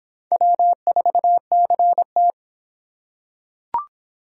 Morse code, heard as a keyed tone, about 26 wpm, 700 Hz. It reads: W4CT